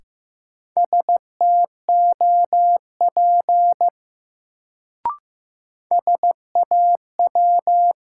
Morse code, heard as a keyed tone, about 15 wpm, 700 Hz. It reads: STOP SAW
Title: Sentences from Top 300 Words - New Word Drill - 15wpm